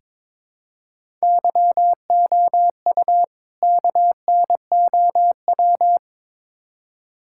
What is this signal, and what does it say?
Morse code, heard as a keyed tone, about 22 wpm, 700 Hz. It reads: YOU KNOW
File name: practice / 22wpm / Build A Phrase - 5-Word Phrases - 22wpm